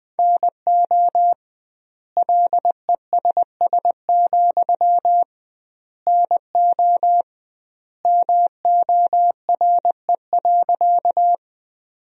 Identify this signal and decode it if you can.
Morse code, heard as a keyed tone, about 20 wpm, 700 Hz. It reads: NO LESS, NO MORE.
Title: 2x Word-Spacing - Sentences from Top 400 Words - 20wpm